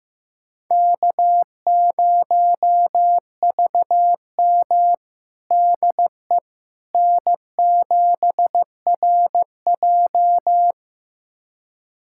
Morse code, heard as a keyed tone, about 15 wpm, 700 Hz. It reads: K0VM DE N7RJ
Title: USA Callsign DE Callsign - 4 Characters - 15wpm